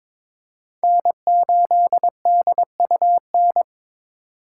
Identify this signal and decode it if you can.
Morse code, heard as a keyed tone, about 22 wpm, 700 Hz. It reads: N8DUN
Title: USA Call Signs - 5 Characters - 2x Character Spacing - 22wpm